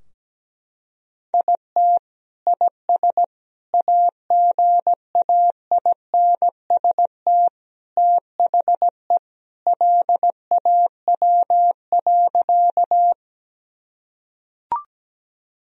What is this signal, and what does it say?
Morse code, heard as a keyed tone, about 17 wpm, 700 Hz. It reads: IT IS AGAINST THE LAW.
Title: Sentences from Top 1000 Words - Review Part 3 - 17wpm